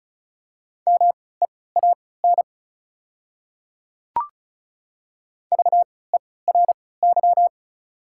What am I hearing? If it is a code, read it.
Morse code, heard as a keyed tone, about 36 wpm, spaced as for 20 wpm, 700 Hz. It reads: MEAN VERY
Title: 3x Character Spacing - Top 200 Words - 35wpm